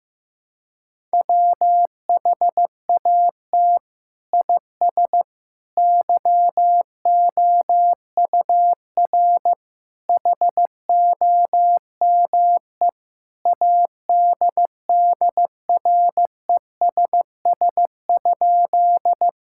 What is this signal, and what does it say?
Morse code, heard as a keyed tone, about 15 wpm, 700 Hz. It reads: WHAT IS YOUR HOME ADDRESS?
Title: Sentences from Top 1000 Words - Review Part 6 - 15wpm